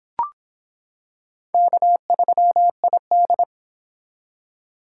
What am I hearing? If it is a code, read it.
Morse code, heard as a keyed tone, about 26 wpm, 700 Hz. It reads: K3ID